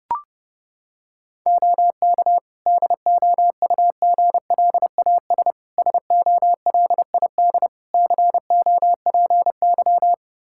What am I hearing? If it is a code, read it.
Morse code, heard as a keyed tone, about 30 wpm, 700 Hz. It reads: OK DOUGLAS SOLID COPY
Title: Practice QSO - Exchange 3 - More Info - 30wpm